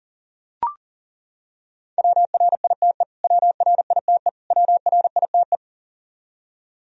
Morse code, heard as a keyed tone, about 40 wpm, 700 Hz. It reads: WRITE WRITE WRITE
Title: Top 100 Words Repeated 40wpm